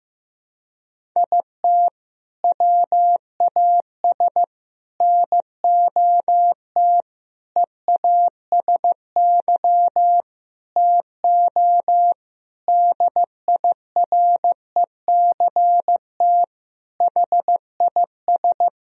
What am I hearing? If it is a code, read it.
Morse code, heard as a keyed tone, about 15 wpm, 700 Hz. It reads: IT WAS NOT EASY TO DIRECT HIS
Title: Sentences from Top 1000 Words - Review Part 2 - 15wpm